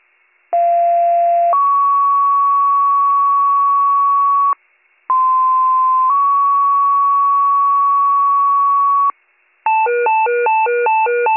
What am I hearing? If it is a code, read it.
Two-tone page, heard as a one-second tone, then a three-second tone, about 690 then 1120 Hz, about 1040 then 1120 Hz. It is heard from 0.5 to 9.1 s.